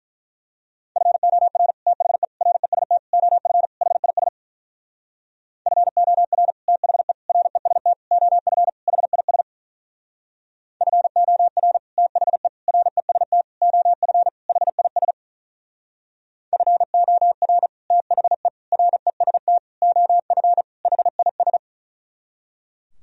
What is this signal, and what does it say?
Morse code, heard as a keyed tone, about 44 wpm, 700 Hz. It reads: FOR THE REST OF HIS FOR THE REST OF HIS FOR THE REST OF HIS FOR THE REST OF HIS